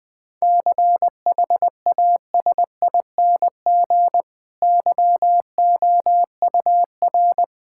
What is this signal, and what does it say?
Morse code, heard as a keyed tone, about 20 wpm, 700 Hz. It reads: CHASING YOUR
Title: Difficult Sentences with Hints - 20wpm